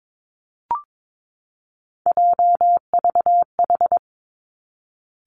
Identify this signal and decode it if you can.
Morse code, heard as a keyed tone, about 22 wpm, 700 Hz. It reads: JVH
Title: Three Challenging Letter-Number Combinations 22wpm